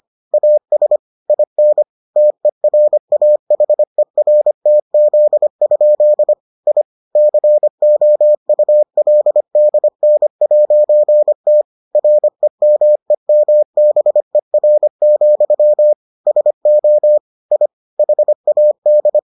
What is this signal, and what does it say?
Morse code, heard as a keyed tone, about 25 wpm, 600 Hz. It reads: AS IN TERAHERTZ? I COULDN'T REMEMBER, SO I HAD